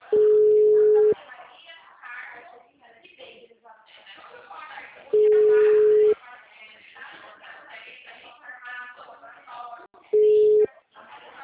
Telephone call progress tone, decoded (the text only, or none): ringback tone